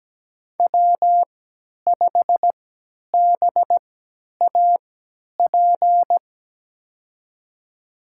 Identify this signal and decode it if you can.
Morse code, heard as a keyed tone, about 17 wpm, spaced as for 10 wpm, 700 Hz. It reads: W5BAP
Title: USA Call Signs - 5 Characters - 3x Character Spacing - 17wpm